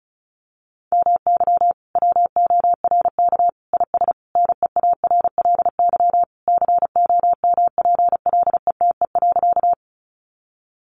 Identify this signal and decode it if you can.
Morse code, heard as a keyed tone, about 34 wpm, 700 Hz. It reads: MY WORK IS NEARLY COMPLETE.
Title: Sentences from Top 1300 Words 35wpm